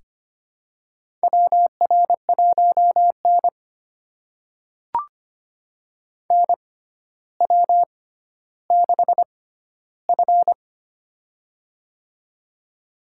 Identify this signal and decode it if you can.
Morse code, heard as a keyed tone, about 25 wpm, spaced as for 13 wpm, 700 Hz. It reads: WR1N NW6F
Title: USA Call Signs - 4 Characters - 6x Character Spacing - 25wpm